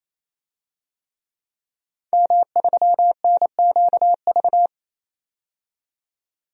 Morse code, heard as a keyed tone, about 28 wpm, 700 Hz. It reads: M3NQV
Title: DXCC-100 Call Signs - 5 Characters - 28wpm